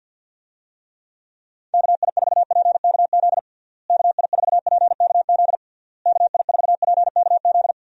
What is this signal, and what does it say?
Morse code, heard as a keyed tone, about 52 wpm, 700 Hz. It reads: KI4PKZ KI4PKZ KI4PKZ
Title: USA Call Signs - 6 Characters - Repeated 3x - 50wpm